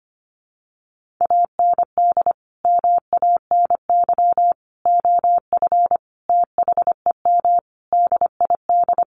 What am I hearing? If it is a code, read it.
Morse code, heard as a keyed tone, about 25 wpm, 700 Hz. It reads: AND MANY OF THEM DID
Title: Sentences from Top 1500 Words 25wpm